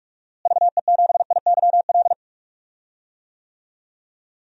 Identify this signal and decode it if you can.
Morse code, heard as a keyed tone, about 41 wpm, 700 Hz. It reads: VE7IYL
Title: DXCC-100 Call Signs - 6 Characters - Repeated 3x - 45wpm